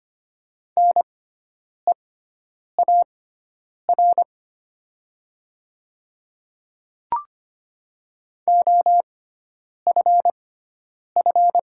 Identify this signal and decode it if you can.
Morse code, heard as a keyed tone, about 25 wpm, spaced as for 9 wpm, 700 Hz. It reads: NEAR OFF
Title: 6x Character Spacing - Top 500 Words - 25wpm